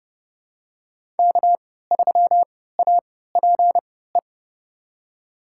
Morse code, heard as a keyed tone, about 32 wpm, spaced as for 17 wpm, 700 Hz. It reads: K3APE
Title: USA Call Signs - 5 Characters - 3x Character Spacing - 30wpm